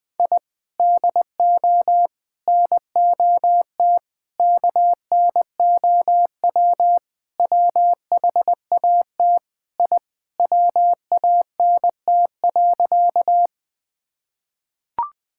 Morse code, heard as a keyed tone, about 19 wpm, 700 Hz. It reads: I DO NOT KNOW WHAT I WANT.
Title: Sentences with 5-Word Phrases - 20wpm